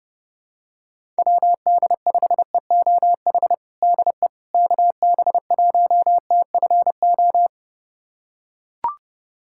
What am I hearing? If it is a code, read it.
Morse code, heard as a keyed tone, about 32 wpm, 700 Hz. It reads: WD5EOH DE KB1TFO
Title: USA Callsign DE Callsign - 6 Characters - 30wpm